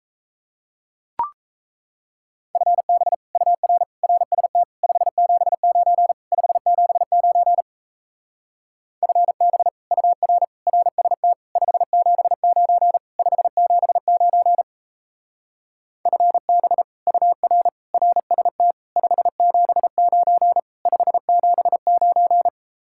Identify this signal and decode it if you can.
Morse code, heard as a keyed tone, about 35 wpm, 700 Hz. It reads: FB UR RST 579 579 FB UR RST 579 579 FB UR RST 579 579
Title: Speed-Racing Common QSO Phrases - 28wpm